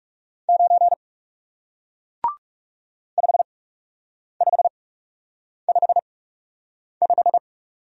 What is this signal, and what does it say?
Morse code, heard as a keyed tone, about 38 wpm, spaced as for 8 wpm, 700 Hz. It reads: 9 5555